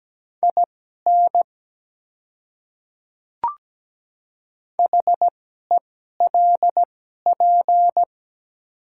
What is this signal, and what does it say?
Morse code, heard as a keyed tone, about 17 wpm, spaced as for 12 wpm, 700 Hz. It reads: IN HELP
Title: 2x Character Spacing - Top 300 Words - 17wpm